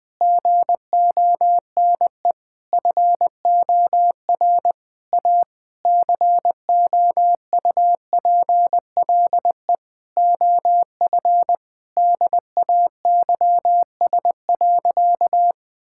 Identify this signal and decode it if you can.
Morse code, heard as a keyed tone, about 20 wpm, 700 Hz. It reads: GONE FOR A COUPLE OF DAYS.
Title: Sentences with 4-Word Phrases - Spoken Beginning - 20wpm